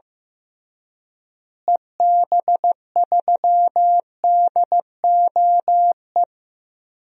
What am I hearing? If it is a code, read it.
Morse code, heard as a keyed tone, about 15 wpm, 700 Hz. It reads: EB3DOE
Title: DXCC-100 Call Signs - 6 Characters - Repeated 3x - 15wpm